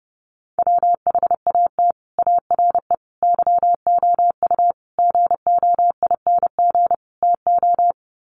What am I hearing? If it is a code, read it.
Morse code, heard as a keyed tone, about 30 wpm, 700 Hz. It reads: WHAT ARE YOU GOING TO
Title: Sets of 5 Words 30wpm